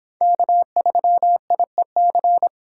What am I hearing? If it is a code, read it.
Morse code, heard as a keyed tone, about 25 wpm, 700 Hz. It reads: K3IEC